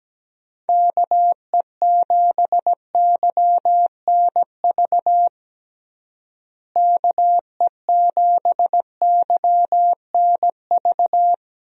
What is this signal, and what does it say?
Morse code, heard as a keyed tone, about 17 wpm, 700 Hz. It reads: KE7YNV KE7YNV
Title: USA Call Signs - 6 Characters - Repeated 3x - 17wpm